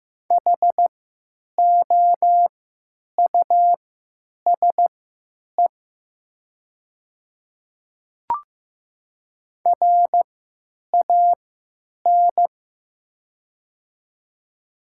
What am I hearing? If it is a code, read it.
Morse code, heard as a keyed tone, about 15 wpm, spaced as for 9 wpm, 700 Hz. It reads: HOUSE RAN
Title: 3x Character Spacing - Top 500 Words - 15wpm